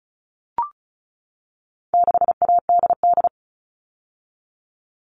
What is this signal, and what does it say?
Morse code, heard as a keyed tone, about 34 wpm, 700 Hz. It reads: 6ADD
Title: All Cut Numbers - 4-Digit 35wpm